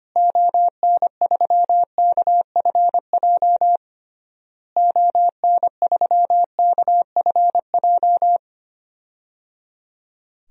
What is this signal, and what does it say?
Morse code, heard as a keyed tone, about 25 wpm, 700 Hz. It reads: ON3KFJ ON3KFJ
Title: DXCC-100 Call Signs - 6 Characters - Repeated 3x - 25wpm